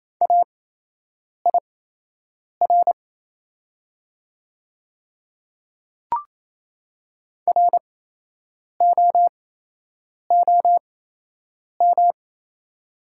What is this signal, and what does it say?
Morse code, heard as a keyed tone, about 28 wpm, spaced as for 8 wpm, 690 Hz. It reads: AIR ROOM
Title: 8x Character Spacing - Top 300 Words - 28wpm